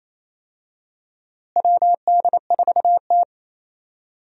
Morse code, heard as a keyed tone, about 28 wpm, 700 Hz. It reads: WD4T
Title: USA Call Signs - 4 Characters - Repeated 3x - 28wpm